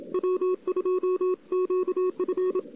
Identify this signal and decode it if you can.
Morse code, heard as a keyed tone, about 27 wpm, 370 Hz. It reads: W2QF
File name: CW / 2008-ss-cw-k5tr-op-n5rz -02:00